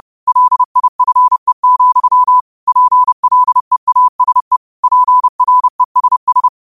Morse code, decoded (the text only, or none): LIFE, PLEASE PRESS